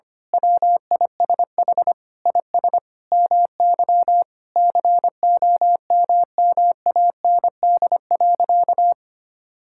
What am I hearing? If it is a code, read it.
Morse code, heard as a keyed tone, about 25 wpm, 700 Hz. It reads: WISH IS MY COMMAND.
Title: Sentences from Top 1600 Words 25wpm